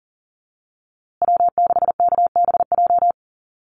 Morse code, heard as a keyed tone, about 39 wpm, 700 Hz. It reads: W6KBJ
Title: USA Call Signs Repeated 3x 40wpm